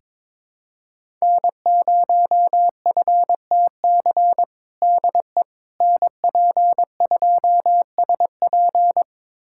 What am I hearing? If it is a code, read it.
Morse code, heard as a keyed tone, about 21 wpm, 700 Hz. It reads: N0FTC DE NP2SP
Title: USA Callsign DE Callsign - 5 Characters - 22wpm